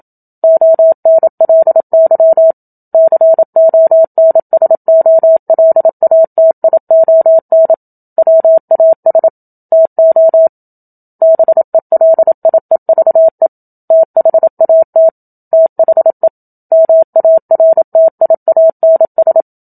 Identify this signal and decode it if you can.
Morse code, heard as a keyed tone, about 27 wpm, 650 Hz. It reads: ONLY CONSOLATION WAS TO BELIEVE THAT THE MARTIANS
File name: Book Two / 01-Chapter 1 - 27 WPM